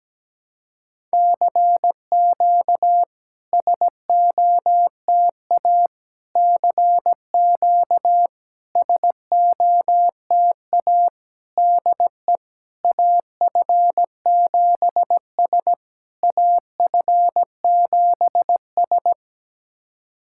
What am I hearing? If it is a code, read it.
Morse code, heard as a keyed tone, about 17 wpm, 690 Hz. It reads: CQ SOTA CQ SOTA DE AF7S AF7S
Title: SOTA QSO Practice 17wpm